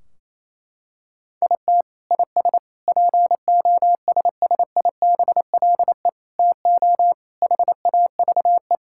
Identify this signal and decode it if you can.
Morse code, heard as a keyed tone, about 27 wpm, 700 Hz. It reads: IT IS POSSIBLE TO HAVE
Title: Sentences from Top 1600 Words 28wpm